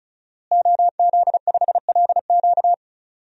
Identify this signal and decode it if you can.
Morse code, heard as a keyed tone, about 34 wpm, 700 Hz. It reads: OZ5LQ
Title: DXCC-100 Call Signs - 5 Characters - Repeated 3x - 35wpm